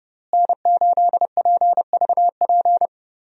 Morse code, heard as a keyed tone, about 32 wpm, 700 Hz. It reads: N8PVP
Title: USA Call Signs - 5 Characters - Repeated 3x - 30wpm